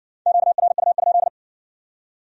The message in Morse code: QRU?

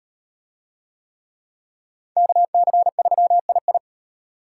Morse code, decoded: KC3II

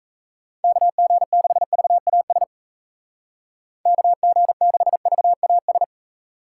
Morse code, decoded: KG6VAS KG6VAS

